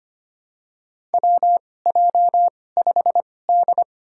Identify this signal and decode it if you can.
Morse code, heard as a keyed tone, about 25 wpm, spaced as for 18 wpm, 700 Hz. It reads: WJ5D
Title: USA Call Signs - 4 Characters - 2x Character Spacing - 25wpm